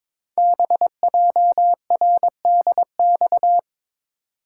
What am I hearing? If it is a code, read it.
Morse code, heard as a keyed tone, about 23 wpm, 700 Hz. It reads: BJRDX
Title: Code Groups - 5 Letters - 22wpm